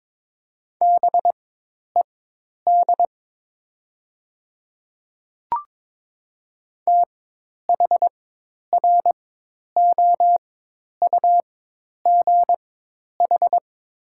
Morse code, heard as a keyed tone, about 23 wpm, spaced as for 10 wpm, 700 Hz. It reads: BED THROUGH
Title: 4x Character Spacing - Top 500 Words - 22wpm